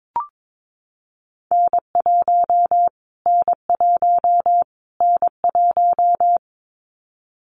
Morse code, heard as a keyed tone, about 22 wpm, 700 Hz. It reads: N1 N1 N1
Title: Callsign Prefixes for Top 25 DXCC - Repeated 3x - 22wpm